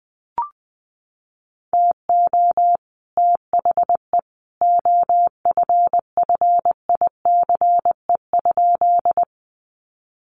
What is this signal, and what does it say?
Morse code, heard as a keyed tone, about 19 wpm, 700 Hz. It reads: TO THE OFFICE?
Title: Sentences from Top 1000 Words 20wpm